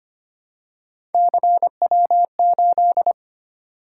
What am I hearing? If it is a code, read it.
Morse code, heard as a keyed tone, about 25 wpm, 700 Hz. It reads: CW8
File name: Three Letter-Number Combinations - 25wpm